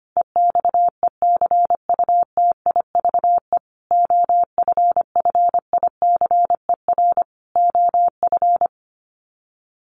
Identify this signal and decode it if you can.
Morse code, heard as a keyed tone, about 25 wpm, 700 Hz. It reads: EXECUTIVE OFFICER OF
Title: Sets of 5 Words Encore2 25wpm